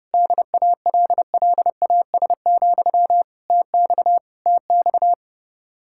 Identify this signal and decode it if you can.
Morse code, heard as a keyed tone, about 30 wpm, 700 Hz. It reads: DALLAS, TX TX